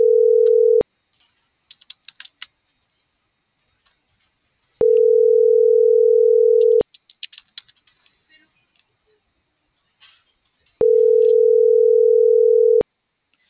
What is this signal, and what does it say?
Ringback tone (North America), 440 + 480 Hz.